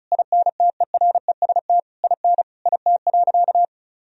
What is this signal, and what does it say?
Morse code, heard as a keyed tone, about 34 wpm, 700 Hz. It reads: INTEREST IN IT.
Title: Sentences from Top 1000 Words - Review Part 4 - 35wpm